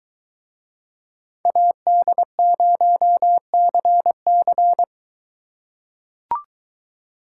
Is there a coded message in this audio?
Yes: Morse code from 1.5 to 4.8 s, heard as a keyed tone, about 21 wpm, 700 Hz. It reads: AD0CC